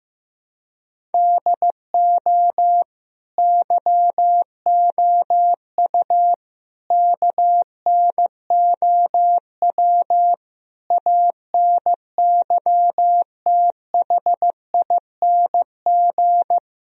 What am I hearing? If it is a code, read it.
Morse code, heard as a keyed tone, about 15 wpm, 700 Hz. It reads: DO YOU KNOW ANYTHING
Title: Sentences from Top 1000 Words - Review Part 1 - 15wpm